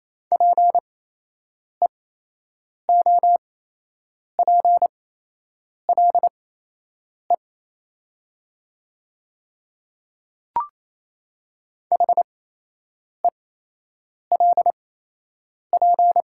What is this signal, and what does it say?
Morse code, heard as a keyed tone, about 28 wpm, spaced as for 8 wpm, 700 Hz. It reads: PEOPLE HELP
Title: 8x Character Spacing - Top 200 Words - 28wpm